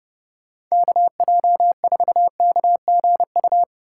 Morse code, heard as a keyed tone, about 36 wpm, 700 Hz. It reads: KJ4KGU
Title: Speed Racing - USA Call Signs 30wpm